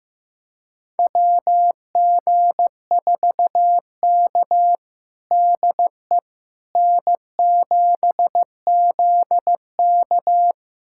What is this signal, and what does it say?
Morse code, heard as a keyed tone, about 15 wpm, 700 Hz. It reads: WG4K DE N7ZK
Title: USA Callsign DE Callsign - 4 Characters - 15wpm